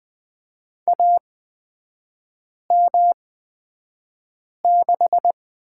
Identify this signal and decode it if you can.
Morse code, heard as a keyed tone, about 20 wpm, spaced as for 5 wpm, 700 Hz. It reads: AM6
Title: 224 Lesson 36e - BT - Three Characters 20wpm